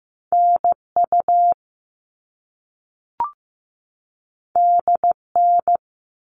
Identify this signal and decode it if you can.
Morse code, heard as a keyed tone, about 15 wpm, 700 Hz. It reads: NU DN
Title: All Cut Numbers - 2-Digit 15wpm